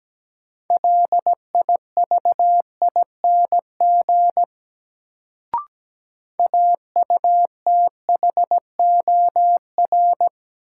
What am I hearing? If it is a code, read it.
Morse code, heard as a keyed tone, about 17 wpm, 700 Hz. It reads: LIVING AUTHOR
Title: Sets of 4 Words - New Word Drill - 17wpm